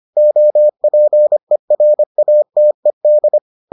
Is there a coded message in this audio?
Yes: Morse code from 0.2 to 3.4 s, heard as a keyed tone, about 25 wpm, 600 Hz. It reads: OPERATED